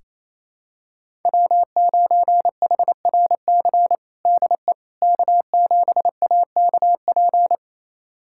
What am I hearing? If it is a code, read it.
Morse code, heard as a keyed tone, about 27 wpm, 700 Hz. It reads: W9HRC DE K7AKP
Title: USA Callsign DE Callsign - 5 Characters - 28wpm